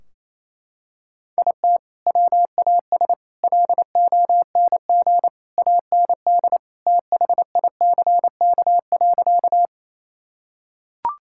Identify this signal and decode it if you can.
Morse code, heard as a keyed tone, about 28 wpm, 700 Hz. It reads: IT WAS LONG AND THICK.
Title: Sentences from Top 1600 Words 28wpm